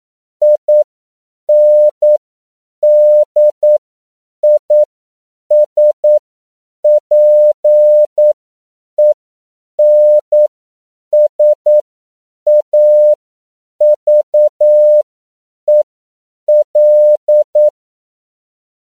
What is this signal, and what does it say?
Morse code, heard as a keyed tone, about 9 wpm, spaced as for 7 wpm, 600 Hz. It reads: INDISPENSAVEL
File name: CW / 10PPM